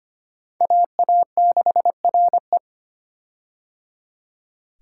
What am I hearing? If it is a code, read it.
Morse code, heard as a keyed tone, about 25 wpm, 700 Hz. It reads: AA6RE
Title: K1USN SST QSO Practice - 25wpm